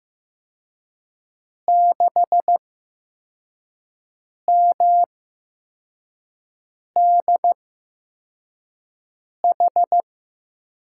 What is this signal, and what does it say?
Morse code, heard as a keyed tone, about 15 wpm, spaced as for 4 wpm, 700 Hz. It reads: 6MDH
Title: USA Call Signs - 6 Characters - 8x Character Spacing - 15wpm